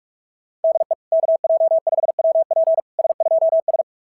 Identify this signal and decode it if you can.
Morse code, heard as a keyed tone, about 46 wpm, 650 Hz. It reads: DE KJ5WP S2S
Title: SOTA QSO Practice 45wpm